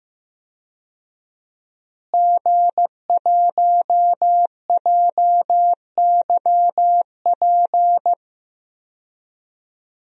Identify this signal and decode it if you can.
Morse code, heard as a keyed tone, about 15 wpm, 700 Hz. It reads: G1JYP
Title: DXCC-100 Call Signs - 5 Characters - Repeated 3x - 15wpm